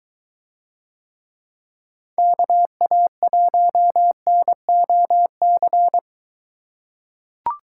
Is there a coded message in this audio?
Yes: Morse code from 2.2 to 6.0 s, heard as a keyed tone, about 22 wpm, 700 Hz. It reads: KA1NOC